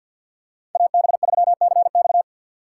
Morse code, heard as a keyed tone, about 53 wpm, 700 Hz. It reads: AB3CX
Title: USA Call Signs - 2x Character Spacing - 50wpm